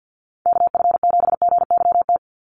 Morse code, heard as a keyed tone, about 48 wpm, 700 Hz. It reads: KF7GYT
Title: USA Call Signs 50wpm